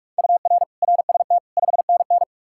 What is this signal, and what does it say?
Morse code, heard as a keyed tone, about 58 wpm, 700 Hz. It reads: UR RST 5NN